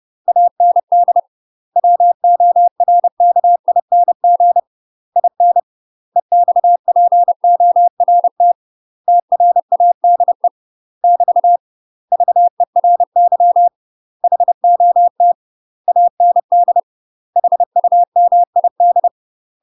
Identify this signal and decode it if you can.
Morse code, heard as a keyed tone, about 30 wpm, 700 Hz. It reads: AND WORKING IN EXPORT TRADE <BT> VERY HOT AND HUMID
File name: UK MASTER QSOs TRAINING / 30WPM / UK MASTER QSOs 0023